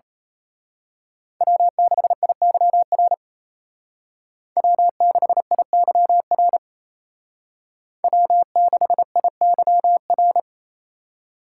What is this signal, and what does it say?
Morse code, heard as a keyed tone, about 30 wpm, 700 Hz. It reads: W6IYR W6IYR W6IYR
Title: Speed Racing - USA 1x3 Call Signs 28wpm